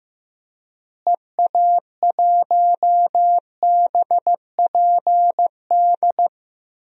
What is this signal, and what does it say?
Morse code, heard as a keyed tone, about 15 wpm, 700 Hz. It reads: EA1BPD
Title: DXCC-100 Call Signs - 6 Characters - Repeated 2x - 15wpm